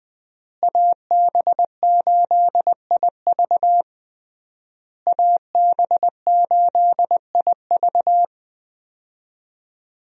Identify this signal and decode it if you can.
Morse code, heard as a keyed tone, about 20 wpm, 700 Hz. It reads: AB8IV AB8IV
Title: USA Call Signs - 5 Characters - Repeated 3x - 20wpm